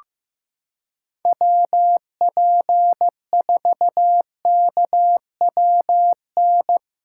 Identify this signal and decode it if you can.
Morse code, heard as a keyed tone, about 15 wpm, 700 Hz. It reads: WP4KWN
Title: USA Callsign DE Callsign - 6 Characters - 15wpm